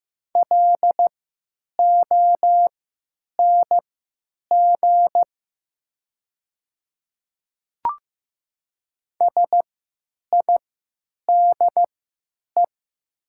Morse code, heard as a keyed tone, about 15 wpm, spaced as for 9 wpm, 700 Hz. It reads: LONG SIDE
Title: 3x Character Spacing - Top 100 Words - 15wpm